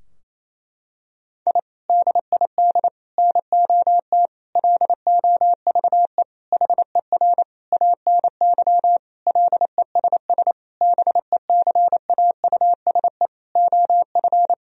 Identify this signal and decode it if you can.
Morse code, heard as a keyed tone, about 28 wpm, 700 Hz. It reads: I DID NOT LOVE HER ANY LESS BECAUSE OF
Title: Sentences from Top 1000 Words - Review Part 2 - 28wpm